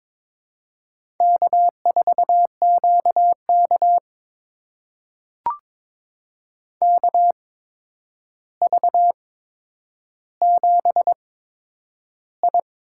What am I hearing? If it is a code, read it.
Morse code, heard as a keyed tone, about 22 wpm, spaced as for 9 wpm, 700 Hz. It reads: K4QK KV7I
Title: USA Call Signs - 4 Characters - 8x Character Spacing - 22wpm